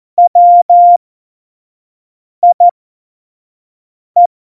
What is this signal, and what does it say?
Morse code, heard as a keyed tone, about 14 wpm, spaced as for 5 wpm, 700 Hz. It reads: WIE
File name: Morsekurs / Witze / Witz 30. April 2019, 25 F70